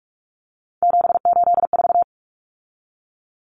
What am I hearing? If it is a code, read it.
Morse code, heard as a keyed tone, about 44 wpm, 700 Hz. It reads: 784
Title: Common Cut Numbers - 3-Digit 45wpm